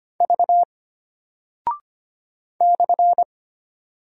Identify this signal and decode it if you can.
Morse code, heard as a keyed tone, about 25 wpm, 700 Hz. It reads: V /
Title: Single Character (Letter:Number:Punctuation:Prosign) Alt 25wpm